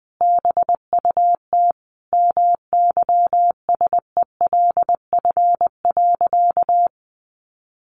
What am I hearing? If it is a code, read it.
Morse code, heard as a keyed tone, about 20 wpm, 700 Hz. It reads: BUT MYSELF.
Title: Sentences from Top 1300 Words 20wpm